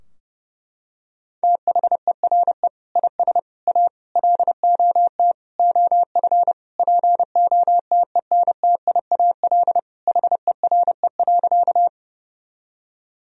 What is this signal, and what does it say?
Morse code, heard as a keyed tone, about 32 wpm, 700 Hz. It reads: THERE IS A LOT OF POTENTIAL HERE.
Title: Sentences from Top 1000 Words - Review Part 4 - 30wpm